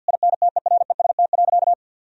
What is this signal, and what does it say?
Morse code, heard as a keyed tone, about 50 wpm, 700 Hz. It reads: INTEREST.
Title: Sentences from Top 1000 Words - Review Part 4 - 50wpm